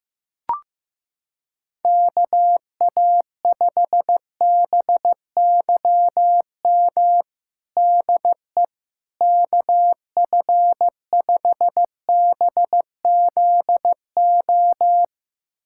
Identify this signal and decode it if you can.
Morse code, heard as a keyed tone, about 15 wpm, 700 Hz. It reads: KA5BYM DE KF5BZO
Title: USA Callsign DE Callsign 15wpm